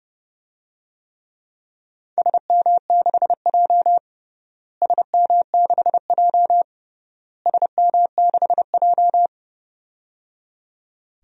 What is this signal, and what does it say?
Morse code, heard as a keyed tone, about 30 wpm, 700 Hz. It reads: SM6J SM6J SM6J
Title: DXCC-100 Call Signs - 4 Characters - Repeated 3x - 30wpm